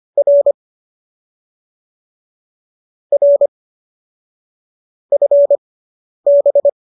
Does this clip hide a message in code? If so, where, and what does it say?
Yes: Morse code from 0.2 to 6.7 s, heard as a keyed tone, about 25 wpm, spaced as for 10 wpm, 570 Hz. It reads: R R FB